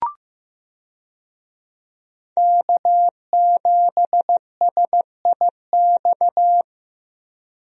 Morse code, heard as a keyed tone, about 15 wpm, 700 Hz. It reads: K7SIX
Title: USA Call Signs - 5 Characters - Repeated 2x - 15wpm